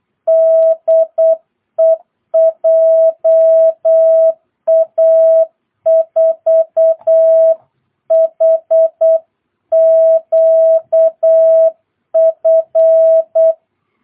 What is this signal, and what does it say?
Morse code, heard as a keyed tone, about 8 wpm, 660 Hz. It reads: DE JA4HQF
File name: jh8cbh / 2022cw